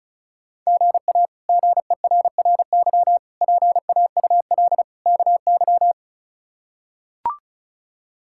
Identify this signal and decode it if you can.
Morse code, heard as a keyed tone, about 34 wpm, 700 Hz. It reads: GA GERRY PAUL KY
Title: K1USN SST QSO Practice - 35wpm